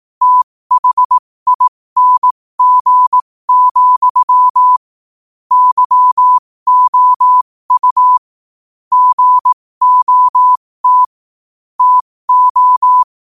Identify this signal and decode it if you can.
Morse code, heard as a keyed tone, about 18 wpm, spaced as for 15 wpm, 1000 Hz. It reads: THING, YOU GOT TO
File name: qotdfiles / 15wpm